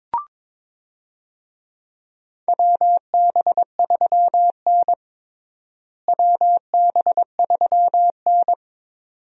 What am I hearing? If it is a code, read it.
Morse code, heard as a keyed tone, about 22 wpm, 700 Hz. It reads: WB3N WB3N
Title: USA Call Signs - 4 Characters - Repeated 2x - 22wpm